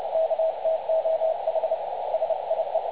Morse code, heard as a keyed tone, about 30 wpm, 650 Hz, with radio noise. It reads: KTOH SIE
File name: dlazaawansowanych / nagraniazpasma / websdr recording 3565.0kHz